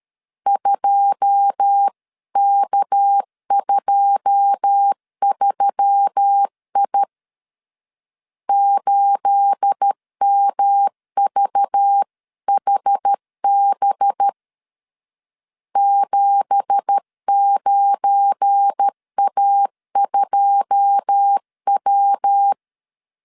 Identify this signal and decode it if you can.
Morse code, heard as a keyed tone, about 13 wpm, 800 Hz. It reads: 2K23I 8MVHB 79A2W
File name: CW / DET / 1 9 1500C S B 12wpm 3819